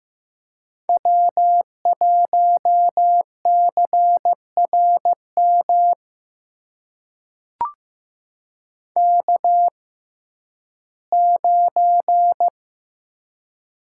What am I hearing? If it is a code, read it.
Morse code, heard as a keyed tone, about 15 wpm, spaced as for 11 wpm, 700 Hz. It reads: W1CRM K9
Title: USA Call Signs - 6x Character Spacing - 15wpm